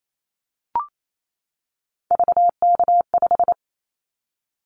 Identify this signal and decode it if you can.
Morse code, heard as a keyed tone, about 28 wpm, 700 Hz. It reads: VK5
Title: Callsign Prefixes for Top 100 DXCC - 28wpm